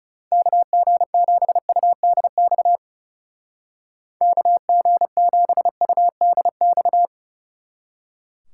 Morse code, heard as a keyed tone, about 40 wpm, 700 Hz. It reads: KG7UDX KG7UDX